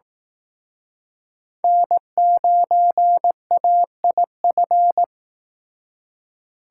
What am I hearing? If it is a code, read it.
Morse code, heard as a keyed tone, about 18 wpm, 700 Hz. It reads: N9AIF
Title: Speed Racing - USA 1x3 Call Signs 15wpm